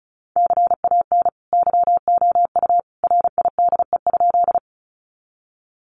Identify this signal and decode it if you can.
Morse code, heard as a keyed tone, about 34 wpm, 700 Hz. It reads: CAN YOU RIDE?
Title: Sentences from Top 1200 Words 35wpm